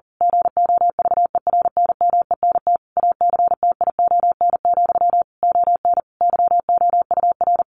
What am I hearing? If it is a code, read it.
Morse code, heard as a keyed tone, about 40 wpm, 700 Hz. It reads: GOVERNMENT ACTION, ON YOUR
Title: A Sound Of Thunder 40wpm